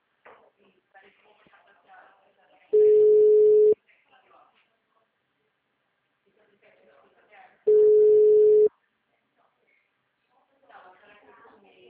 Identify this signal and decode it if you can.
Ringback tone (Europe, 425 Hz), with speech.